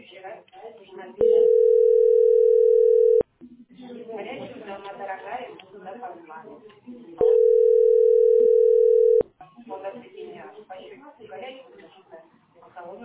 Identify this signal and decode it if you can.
Ringback tone (North America), 440 + 480 Hz, with speech.
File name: ce6f425e-19e6-43f5-a331